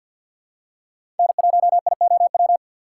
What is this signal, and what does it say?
Morse code, heard as a keyed tone, about 52 wpm, 700 Hz. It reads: N1IOW
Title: USA Call Signs - 5 Characters - 2x Character Spacing - 50wpm